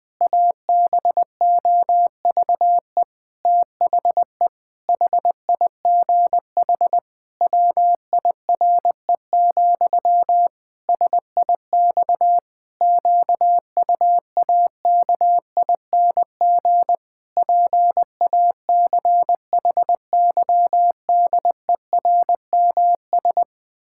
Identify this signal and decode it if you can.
Morse code, heard as a keyed tone, about 20 wpm, 700 Hz. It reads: ABOVE THE HIGH WIRE, SIX QUAKING PACHYDERMS
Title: Pangram Challenge v2 - 20wpm